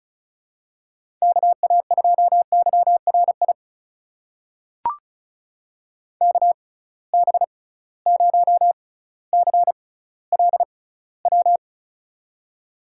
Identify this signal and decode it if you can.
Morse code, heard as a keyed tone, about 36 wpm, spaced as for 18 wpm, 700 Hz. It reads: KA2YRI KB0CLW
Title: USA Call Signs - 6 Characters - 6x Character Spacing - 35wpm